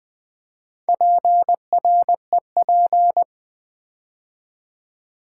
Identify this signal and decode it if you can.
Morse code, heard as a keyed tone, about 20 wpm, 700 Hz. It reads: PREP
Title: Build A Word - 2 Letters At A Time - 8-Letter Words - 20wpm